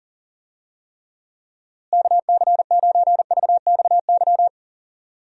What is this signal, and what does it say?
Morse code, heard as a keyed tone, about 40 wpm, 700 Hz. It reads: KC9VXY
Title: USA Call Signs - 6 Characters - Repeated 2x - 40wpm